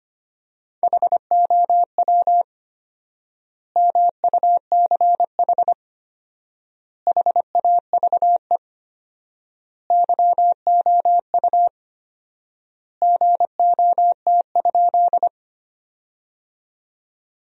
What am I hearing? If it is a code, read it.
Morse code, heard as a keyed tone, about 25 wpm, 700 Hz. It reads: HOW MUCH HAVE YOU GOT?
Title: 4x Word-Spacing - Sentences from Top 300 Words - 25wpm